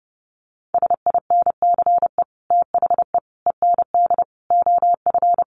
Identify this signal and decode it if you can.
Morse code, heard as a keyed tone, about 30 wpm, 700 Hz. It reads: SINCE THE END OF
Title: Sets of 5 Words Encore 30wpm